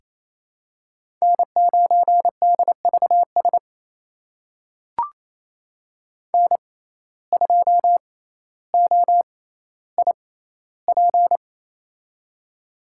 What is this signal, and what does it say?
Morse code, heard as a keyed tone, about 28 wpm, spaced as for 14 wpm, 700 Hz. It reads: N9DVS N2OIP
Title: USA Call Signs - 5 Characters - 6x Character Spacing - 28wpm